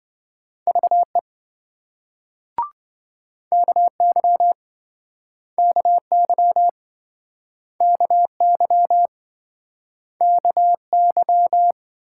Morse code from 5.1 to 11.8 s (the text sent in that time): KY KY KY